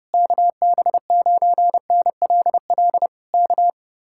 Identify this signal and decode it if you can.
Morse code, heard as a keyed tone, about 32 wpm, 700 Hz. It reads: KB9NLL K